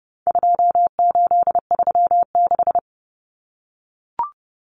Morse code, heard as a keyed tone, about 30 wpm, 700 Hz. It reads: 2836
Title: Common Cut Numbers - 4-Digit 30wpm